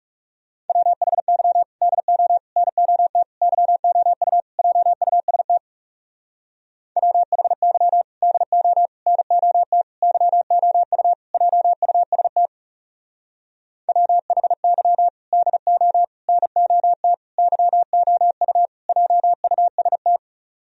Morse code, heard as a keyed tone, about 40 wpm, 700 Hz. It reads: WHY DO NOT YOU JUST WHY DO NOT YOU JUST WHY DO NOT YOU JUST